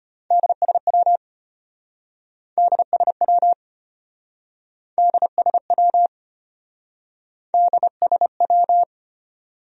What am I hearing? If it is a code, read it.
Morse code, heard as a keyed tone, about 33 wpm, 700 Hz. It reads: DSW DSW DSW DSW